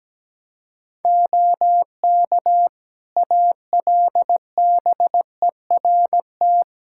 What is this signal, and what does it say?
Morse code, heard as a keyed tone, about 17 wpm, 700 Hz. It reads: OK ALBERT
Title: Practice QSO - Exchange 3 - More Info - 17wpm